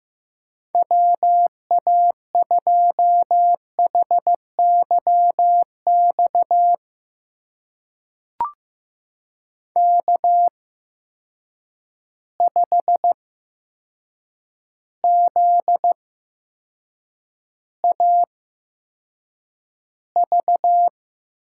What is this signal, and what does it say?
Morse code, heard as a keyed tone, about 15 wpm, spaced as for 7 wpm, 700 Hz. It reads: WA2HYX K5ZAV
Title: USA Call Signs - 8x Character Spacing - 15wpm